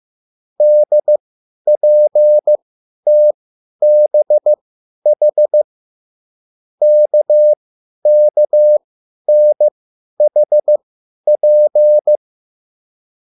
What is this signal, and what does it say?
Morse code, heard as a keyed tone, about 15 wpm, spaced as for 10 wpm, 600 Hz. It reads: DPTBH KKNHP
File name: QSL / CW lesson